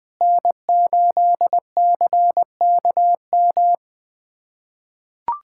Morse code, heard as a keyed tone, about 20 wpm, 700 Hz. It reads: N8CKM
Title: USA Callsign DE Callsign - 5 Characters - 20wpm